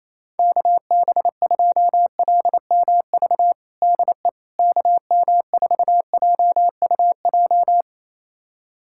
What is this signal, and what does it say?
Morse code, heard as a keyed tone, about 28 wpm, 700 Hz. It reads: KB2LMV DE KM4JUJ
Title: USA Callsign DE Callsign - 6 Characters - 28wpm